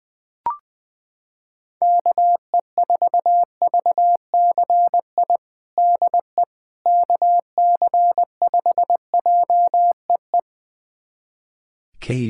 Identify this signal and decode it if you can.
Morse code, heard as a keyed tone, about 20 wpm, 700 Hz. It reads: KE4VCI DE KC5JEE
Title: USA Callsign DE Callsign - 6 Characters - 20wpm